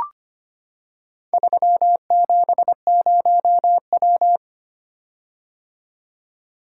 Morse code, heard as a keyed tone, about 25 wpm, 700 Hz. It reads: 370W
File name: practice / 25wpm / Four Letter-Number Combinations - 25wpm